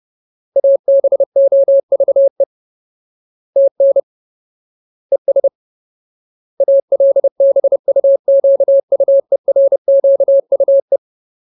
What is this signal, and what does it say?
Morse code, heard as a keyed tone, about 30 wpm, 550 Hz. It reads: ABOVE TN ES ALBUQUERQUE
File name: Ultimate Word List / 30WPM Ultimate